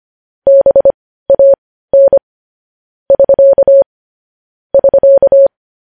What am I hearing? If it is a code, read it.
Morse code, heard as a keyed tone, about 25 wpm, spaced as for 15 wpm, 550 Hz. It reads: BAN <SK> <SK>
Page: news-headlines-in-morse-code-at-15-wpm